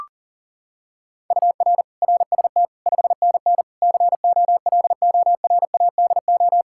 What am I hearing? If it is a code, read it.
Morse code, heard as a keyed tone, about 40 wpm, 700 Hz. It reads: UR RST 5NN COLORADO
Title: Practice QSO - Exchange 1 - Beginning - 40wpm